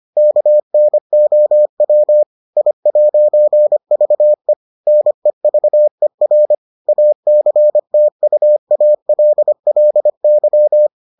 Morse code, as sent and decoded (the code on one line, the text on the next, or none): -.- -. --- .-- / .. .----. ...- . / -. ...- . .-. / .- -.-. - ..- .- .-.. .-.. -.--
KNOW I'VE NEVER ACTUALLY